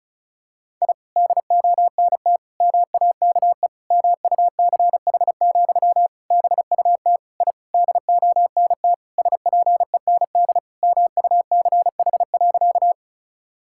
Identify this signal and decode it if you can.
Morse code, heard as a keyed tone, about 36 wpm, 700 Hz. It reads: I DONT MAKE MUCH, BUT I DONT SPEND MUCH.
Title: Sentences from Top 1000 Words - Review Part 5 - 35wpm